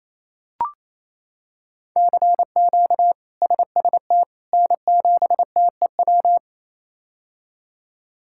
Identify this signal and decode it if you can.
Morse code, heard as a keyed tone, about 28 wpm, 700 Hz. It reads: CQ SST N7TEW
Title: K1USN SST QSO Practice - 28wpm